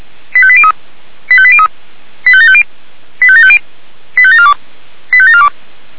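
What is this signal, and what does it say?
Five-tone selective call (ZVEI), heard as quick runs of five tones, tones of about 70 ms. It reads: 85803 85803 85750 85750 85742 85742
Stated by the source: ZVEI five-tone selective call 85827, received by SDR.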